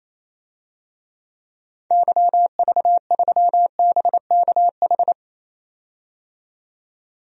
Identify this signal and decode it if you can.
Morse code, heard as a keyed tone, about 28 wpm, 700 Hz. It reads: YV3BKH
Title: DXCC-100 Call Signs - 6 Characters - 28wpm